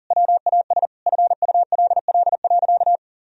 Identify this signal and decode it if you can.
Morse code, heard as a keyed tone, about 40 wpm, 700 Hz. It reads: WAS FULL.
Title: Sentences from Top 1000 Words - Review Part 3 - 40wpm